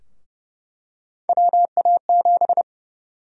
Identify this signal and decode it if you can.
Morse code, heard as a keyed tone, about 30 wpm, 700 Hz. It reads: WA7
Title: USA Call Sign Prefixes - 30wpm